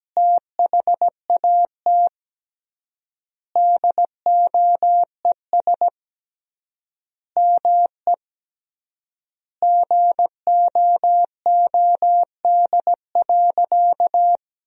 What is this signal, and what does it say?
Morse code, heard as a keyed tone, about 17 wpm, 700 Hz. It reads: THAT DOES ME GOOD.
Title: 3x Word-Spacing - Sentences from Top 200 Words - 17wpm